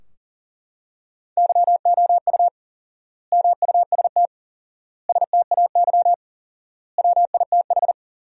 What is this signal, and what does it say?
Morse code, heard as a keyed tone, about 40 wpm, 700 Hz. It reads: YOU MUST STAY WITH
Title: 4x Word-Spacing - Sentences from Top 500 Words - 40wpm